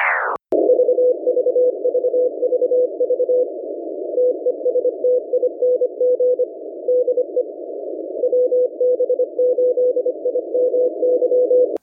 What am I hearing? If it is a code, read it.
Morse code, heard as a keyed tone, about 25 wpm, 500 Hz. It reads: VVVVV TESTING DE WB8IMY